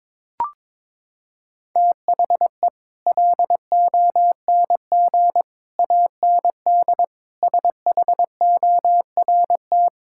Morse code, heard as a keyed tone, about 21 wpm, 700 Hz. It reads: THE LONG AND SHORT
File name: practice / 22wpm / Sets of 8 Words - 22wpm